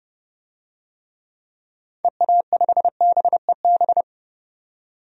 Morse code, heard as a keyed tone, about 32 wpm, 700 Hz. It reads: EA5BEB
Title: DXCC-100 Call Signs - 6 Characters - 30wpm